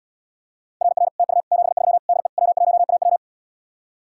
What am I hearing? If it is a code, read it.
Morse code, heard as a keyed tone, about 79 wpm, 700 Hz. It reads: RR TU 73 DE W1AW